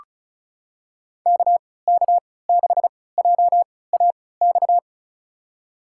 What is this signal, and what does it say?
Morse code, heard as a keyed tone, about 36 wpm, spaced as for 20 wpm, 700 Hz. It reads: KK6JAX
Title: USA Call Signs - 6 Characters - 3x Character Spacing - 35wpm